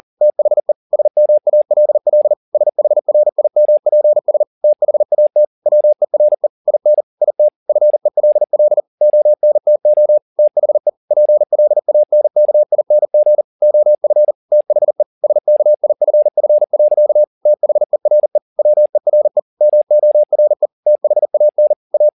Morse code, heard as a keyed tone, about 40 wpm, 600 Hz. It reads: THE SMALL SHRIMPS THAT WERE IN IT FELL ONTO THE PLANKING OF THE SKIFF. THERE WERE MORE THAN A
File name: the old man and the sea 2 40wpm